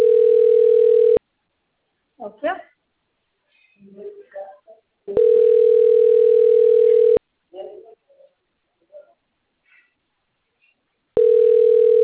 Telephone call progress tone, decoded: ringback tone

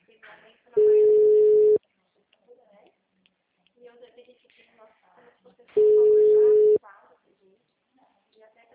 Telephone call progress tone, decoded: ringback tone

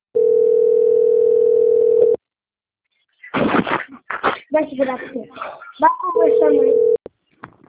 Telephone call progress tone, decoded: ringback tone

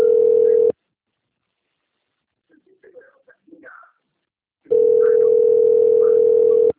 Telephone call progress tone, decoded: ringback tone

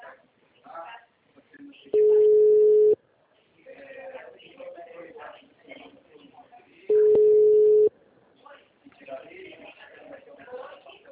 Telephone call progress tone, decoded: ringback tone

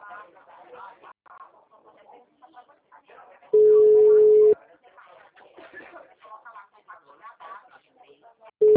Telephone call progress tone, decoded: ringback tone

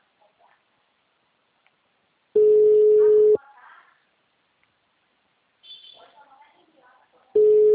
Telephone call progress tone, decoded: ringback tone

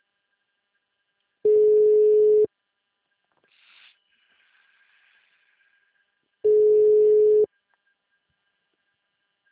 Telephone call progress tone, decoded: ringback tone

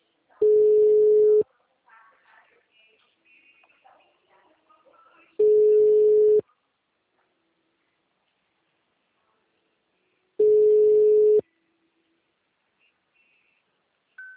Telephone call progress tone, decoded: ringback tone

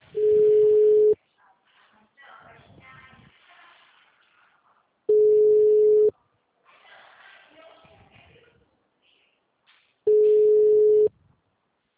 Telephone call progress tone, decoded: ringback tone

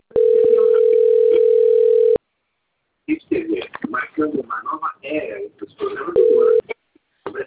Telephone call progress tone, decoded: ringback tone

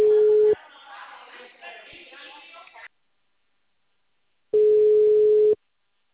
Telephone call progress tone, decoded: ringback tone

